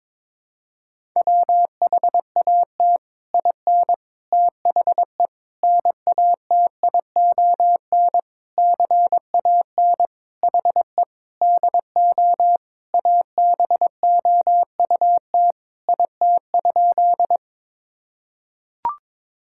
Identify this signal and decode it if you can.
Morse code, heard as a keyed tone, about 22 wpm, 700 Hz. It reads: WHAT IN THE NATION CAN HE DO ABOUT IT?
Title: Sentences from Top 1000 Words - Review Part 6 - 22wpm